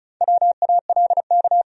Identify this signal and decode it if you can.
Morse code, heard as a keyed tone, about 43 wpm, 700 Hz. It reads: WALK
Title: Speed-Racing Top 300 Words 35wpm